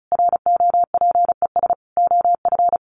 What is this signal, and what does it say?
Morse code, heard as a keyed tone, about 34 wpm, 700 Hz. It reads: ROPES OF
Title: A Sound Of Thunder 35wpm